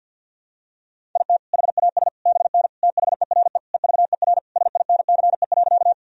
Morse code, heard as a keyed tone, about 51 wpm, 700 Hz. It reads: IT HAS BN THERE EVER SINCE.
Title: Sentences from Top 400 Words and Abbreviations 50wpm